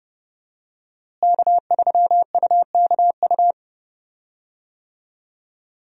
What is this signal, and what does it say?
Morse code, heard as a keyed tone, about 31 wpm, 700 Hz. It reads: K3UKU
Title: USA Call Signs - 5 Characters - Repeated 3x - 30wpm